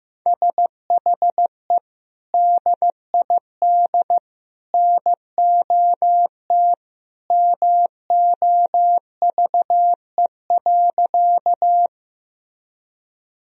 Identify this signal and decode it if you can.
Morse code, heard as a keyed tone, about 15 wpm, 700 Hz. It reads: SHE DID NOT MOVE.
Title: Sentences from Top 1000 Words - Review Part 1 - 15wpm